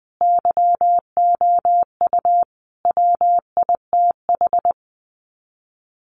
Morse code, heard as a keyed tone, about 20 wpm, 700 Hz. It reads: YOU WITH
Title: Sets of 5 Words Encore2 20wpm